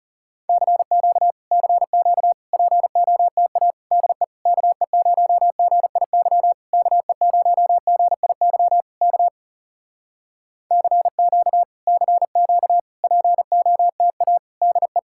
Morse code, heard as a keyed tone, about 37 wpm, 700 Hz. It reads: CQ CQ POTA DE KE0GIY KE0GIY K CQ CQ POTA DE
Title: Speed-Racing Common QSO Phrases Encore - 30wpm